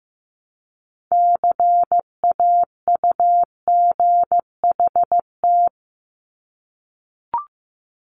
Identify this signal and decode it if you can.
Morse code, heard as a keyed tone, about 15 wpm, 700 Hz. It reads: CAUGHT
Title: Multi-letter Phonemes Repeated 3x 15wpm